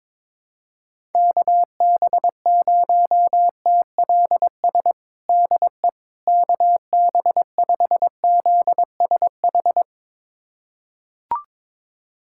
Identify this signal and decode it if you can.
Morse code, heard as a keyed tone, about 22 wpm, 700 Hz. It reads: KB0TLS DE KB5ZSH
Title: USA Callsign DE Callsign - 6 Characters - 22wpm